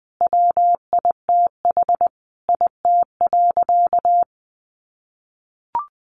Morse code, heard as a keyed tone, about 19 wpm, 700 Hz. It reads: WITH IT.
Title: Sentences from Top 800 Words 20wpm